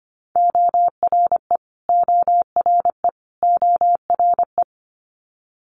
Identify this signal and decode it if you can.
Morse code, heard as a keyed tone, about 25 wpm, 700 Hz. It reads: ORE ORE ORE
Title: Multi-letter Phonemes Repeated 3x 25wpm